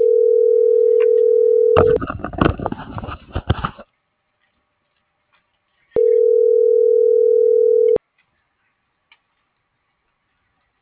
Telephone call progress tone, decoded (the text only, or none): ringback tone